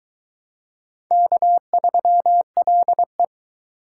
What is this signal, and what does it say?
Morse code, heard as a keyed tone, about 22 wpm, 700 Hz. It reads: K3LE